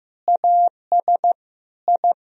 Morse code, heard as a keyed tone, about 15 wpm, 700 Hz. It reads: AS I
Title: Sentences from Top 1000 Words - Review Part 1 - 15wpm